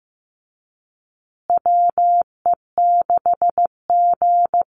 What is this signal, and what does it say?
Morse code, heard as a keyed tone, about 15 wpm, 700 Hz. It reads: WE6G
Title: USA Call Signs Repeated 3x 15wpm